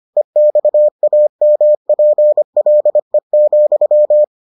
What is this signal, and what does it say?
Morse code, heard as a keyed tone, about 25 wpm, 600 Hz. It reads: EXAMPLE,